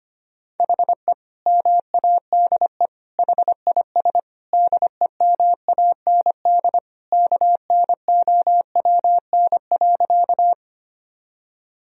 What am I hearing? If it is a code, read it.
Morse code, heard as a keyed tone, about 25 wpm, 700 Hz. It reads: HE MADE HIS DEMAND KNOWN.
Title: Sentences from Top 1000 Words - Review Part 4 - 25wpm